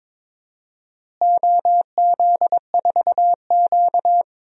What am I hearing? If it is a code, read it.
Morse code, heard as a keyed tone, about 22 wpm, 700 Hz. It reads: OZ4Q
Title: DXCC-100 Call Signs - 4 Characters - Repeated 3x - 22wpm